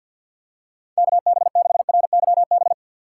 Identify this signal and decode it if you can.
Morse code, heard as a keyed tone, about 47 wpm, 700 Hz. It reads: KB6RYB
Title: USA Call Signs - 6 Characters - 6x Character Spacing - 50wpm